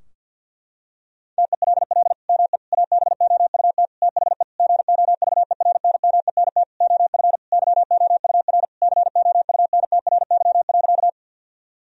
Morse code, heard as a keyed tone, about 50 wpm, 700 Hz. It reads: TELL ME ABOUT THE GOVERNMENT OF YOUR COUNTRY.
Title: Sentences from Top 1000 Words - Review Part 3 - 50wpm